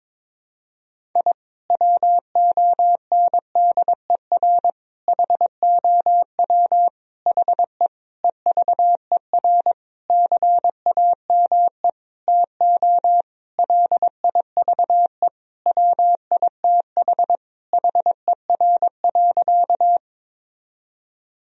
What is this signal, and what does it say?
Morse code, heard as a keyed tone, about 22 wpm, 700 Hz. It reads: I WONDER HOW HE EVER CAME TO LIVE WITH HER.
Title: Sentences from Top 1000 Words - Review Part 3 - 22wpm